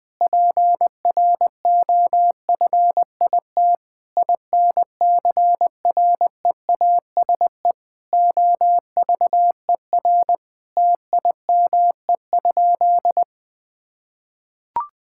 Morse code, heard as a keyed tone, about 21 wpm, 700 Hz. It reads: PROFIT INCREASE OVER TIME?
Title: Sentences from Top 1000 Words - Review Part 5 - 20wpm